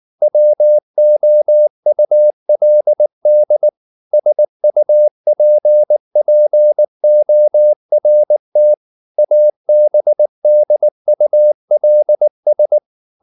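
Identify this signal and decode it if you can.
Morse code, heard as a keyed tone, about 19 wpm, 600 Hz. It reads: WOULD SUPPORT ABDULS
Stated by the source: Morse code at 19 wpm from CW med Gnister, reading Ken Follett's novel Never in English.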